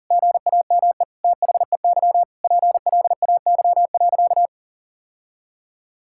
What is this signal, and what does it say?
Morse code, heard as a keyed tone, about 37 wpm, 700 Hz. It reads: GAME THEY PLAY.
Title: Sentences from Top 1000 Words - Review Part 5 - 40wpm